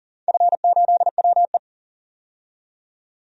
Morse code, heard as a keyed tone, about 40 wpm, 700 Hz. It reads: F8WE
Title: DXCC-100 Call Signs - 4 Characters - 40wpm